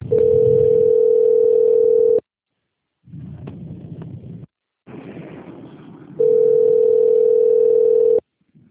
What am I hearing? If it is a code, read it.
Ringback tone (North America), 440 + 480 Hz.
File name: out- - - - - .58410